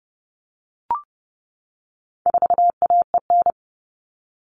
Morse code, heard as a keyed tone, about 30 wpm, 700 Hz. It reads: 4AEN
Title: All Cut Numbers - 4-Digit 30wpm